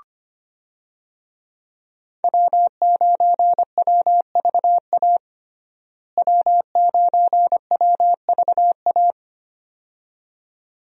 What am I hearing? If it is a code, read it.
Morse code, heard as a keyed tone, about 25 wpm, 700 Hz. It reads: W9WVA W9WVA
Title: USA Call Signs - 5 Characters - Repeated 2x - 25wpm